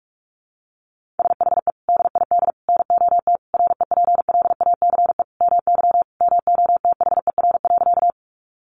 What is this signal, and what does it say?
Morse code, heard as a keyed tone, about 44 wpm, 700 Hz. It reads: SHE DID NOT REPLACE MY MOTHER.
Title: Sentences from Top 1400 Words 45wpm